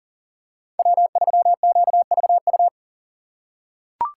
Morse code, heard as a keyed tone, about 40 wpm, 700 Hz. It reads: W3QVU
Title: USA Call Signs - 5 Characters - 4x Character Spacing - 40wpm